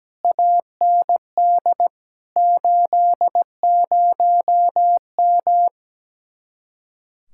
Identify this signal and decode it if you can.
Morse code, heard as a keyed tone, about 17 wpm, 700 Hz. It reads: AND 80M
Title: Speed-Racing Common QSO Phrases - 17wpm